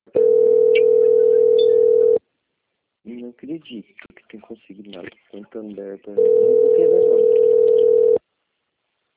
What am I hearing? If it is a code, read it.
Ringback tone (North America), 440 + 480 Hz, with speech.
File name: out- - - - - .22050